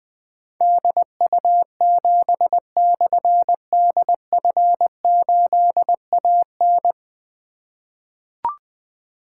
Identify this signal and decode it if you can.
Morse code, heard as a keyed tone, about 21 wpm, 700 Hz. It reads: DU7/DF8AN
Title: Difficult Call Signs 20wpm